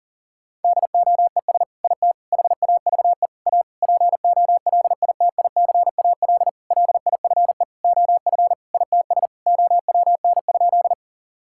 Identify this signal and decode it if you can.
Morse code, heard as a keyed tone, about 37 wpm, 690 Hz. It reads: DOES IT HAVE A POLITICAL LIFE OF ITS OWN?
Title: Sentences from Top 1000 Words - Review Part 5 - 40wpm